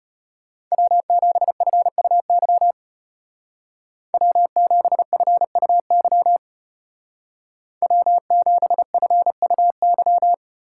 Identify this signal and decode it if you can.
Morse code, heard as a keyed tone, about 33 wpm, 700 Hz. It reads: W7FUY W7FUY W7FUY